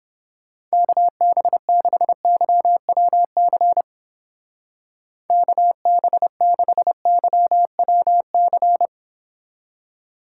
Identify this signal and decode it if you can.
Morse code, heard as a keyed tone, about 30 wpm, 700 Hz. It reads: KB6YWC KB6YWC